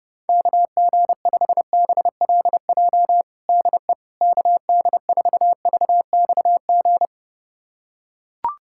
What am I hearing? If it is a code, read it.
Morse code, heard as a keyed tone, about 30 wpm, 700 Hz. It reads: KG5BLJ DE KD4VXG
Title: USA Callsign DE Callsign 30wpm